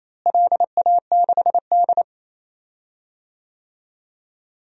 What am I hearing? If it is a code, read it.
Morse code, heard as a keyed tone, about 28 wpm, 700 Hz. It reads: LA6D
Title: DXCC-100 Call Signs - 4 Characters - Repeated 2x - 28wpm